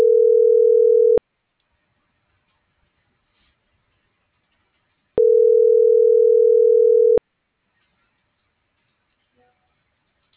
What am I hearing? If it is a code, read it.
Ringback tone (North America), 440 + 480 Hz.